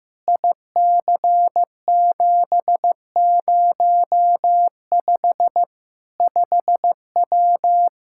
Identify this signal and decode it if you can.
Morse code, heard as a keyed tone, about 15 wpm, 700 Hz. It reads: IC705 5W
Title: K1USN SST QSO Practice 15wpm